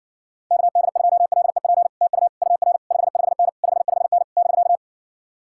Morse code, heard as a keyed tone, about 64 wpm, 700 Hz. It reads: KD2LEP TU UR 55N 55N <BK>